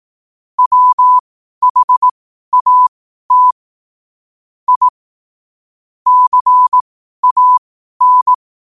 Morse code, heard as a keyed tone, about 18 wpm, spaced as for 12 wpm, 1000 Hz. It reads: WHAT I CAN
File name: qotdfiles / 12wpm